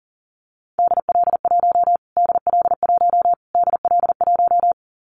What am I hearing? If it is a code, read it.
Morse code, heard as a keyed tone, about 37 wpm, 700 Hz. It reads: DL1 DL1 DL1
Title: Callsign Prefixes for Top 25 DXCC - Repeated 3x - 40wpm